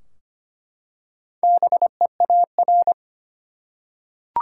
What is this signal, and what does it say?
Morse code, heard as a keyed tone, about 25 wpm, 700 Hz. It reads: BEAR
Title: Four Letter Words - 25wpm